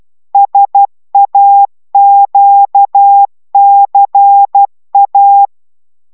Morse code, heard as a keyed tone, about 12 wpm, 800 Hz. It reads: SAQCA